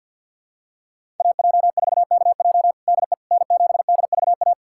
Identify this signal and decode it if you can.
Morse code, heard as a keyed tone, about 50 wpm, 700 Hz. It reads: AJ4KJ DE N7DVA
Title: USA Callsign DE Callsign - 5 Characters - 50wpm